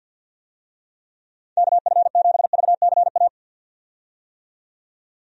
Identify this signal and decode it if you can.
Morse code, heard as a keyed tone, about 52 wpm, 700 Hz. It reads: KF7VCA
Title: USA Call Signs - 6 Characters - 50wpm